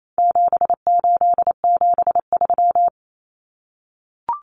Morse code, heard as a keyed tone, about 28 wpm, 700 Hz. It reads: 7873
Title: Common Cut Numbers - 4-Digit 28wpm